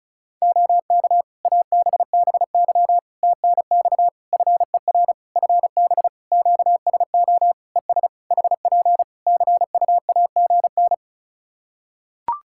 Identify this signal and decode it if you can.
Morse code, heard as a keyed tone, about 34 wpm, 700 Hz. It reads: OK ABBY TNX FER FB QSO ES HP CUAGN
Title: Practice QSO - Exchange 4 - Ending - 35wpm